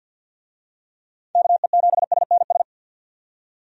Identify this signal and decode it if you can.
Morse code, heard as a keyed tone, about 50 wpm, 700 Hz. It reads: KE7SNS
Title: USA Call Signs - 6 Characters - 50wpm